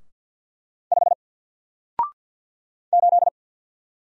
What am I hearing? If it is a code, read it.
Morse code, heard as a keyed tone, about 48 wpm, 700 Hz. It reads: 5 8
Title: Single Character (Letter:Number:Punctuation:Prosign) Alt 50wpm